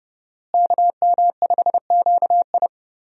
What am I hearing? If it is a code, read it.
Morse code, heard as a keyed tone, about 30 wpm, 700 Hz. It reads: KM5QI
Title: USA Call Signs - 5 Characters - Repeated 3x - 30wpm